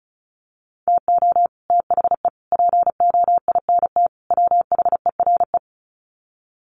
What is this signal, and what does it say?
Morse code, heard as a keyed tone, about 34 wpm, 700 Hz. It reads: TO THE POINT WHERE
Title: Sets of 4 Words 35wpm